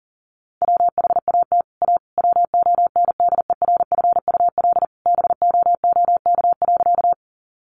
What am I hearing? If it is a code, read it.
Morse code, heard as a keyed tone, about 40 wpm, 700 Hz. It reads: WHAT A WONDERFUL BOOK.
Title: Sentences from Top 1100 Words 40wpm